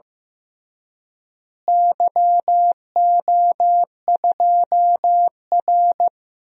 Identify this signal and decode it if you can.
Morse code, heard as a keyed tone, about 15 wpm, 700 Hz. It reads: YO2R
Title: DXCC-100 Call Signs - 4 Characters - Repeated 3x - 15wpm